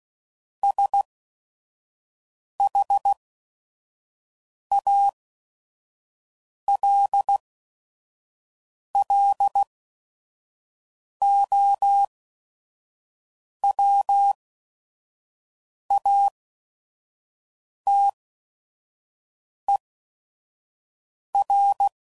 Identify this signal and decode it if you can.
Morse code, heard as a keyed tone, about 16 wpm, spaced as for 5 wpm, 800 Hz. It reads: SHALLOWATER
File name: CW Code Groups @5WPM / fromSM's / Cities